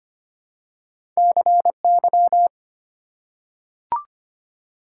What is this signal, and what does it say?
Morse code, heard as a keyed tone, about 25 wpm, 690 Hz. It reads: CY
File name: Two Letter-Number Combinations - 25wpm